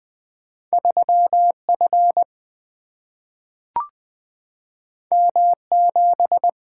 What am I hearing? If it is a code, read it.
Morse code, heard as a keyed tone, about 20 wpm, 700 Hz. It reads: 3 F M 7
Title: Two Letter-Number Combinations - 20wpm